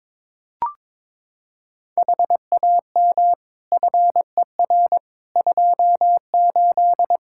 Morse code, heard as a keyed tone, about 22 wpm, 700 Hz. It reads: HAM FER 28
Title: K1USN SST QSO Practice 22wpm